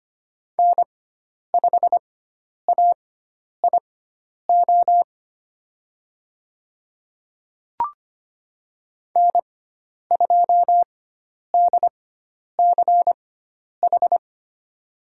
Morse code, heard as a keyed tone, about 25 wpm, spaced as for 10 wpm, 700 Hz. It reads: N5AIO N2DCH
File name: Lesson 16g - Letter c - Call Signs 25wpm